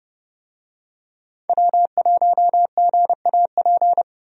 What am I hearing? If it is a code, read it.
Morse code, heard as a keyed tone, about 31 wpm, 700 Hz. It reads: W1GAP
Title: USA Call Signs - 5 Characters - 30wpm